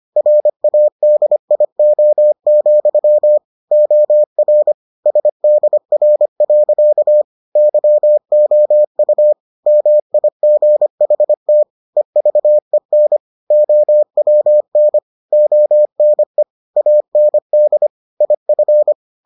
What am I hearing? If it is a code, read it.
Morse code, heard as a keyed tone, about 25 wpm, 600 Hz. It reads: RADIO, OR SDR. YOU MIGHT EVEN OWN ONE AND IF